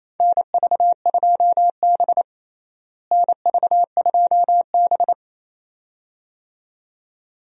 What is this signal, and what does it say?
Morse code, heard as a keyed tone, about 28 wpm, 700 Hz. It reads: NV2B NV2B
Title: USA Call Signs - 4 Characters - Repeated 2x - 28wpm